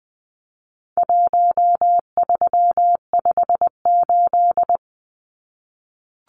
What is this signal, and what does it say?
Morse code, heard as a keyed tone, about 20 wpm, 700 Hz. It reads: 1358
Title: Numbers - Contest Style - 4-Digit 20wpm